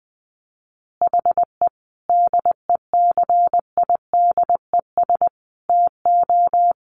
Morse code, heard as a keyed tone, about 20 wpm, 700 Hz. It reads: HE DECIDES TO
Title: Additional Easy Phrases 20wpm